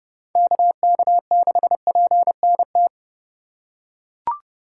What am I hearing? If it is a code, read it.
Morse code, heard as a keyed tone, about 32 wpm, 700 Hz. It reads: KK6PNT
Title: USA Call Signs - 6 Characters - 3x Character Spacing - 30wpm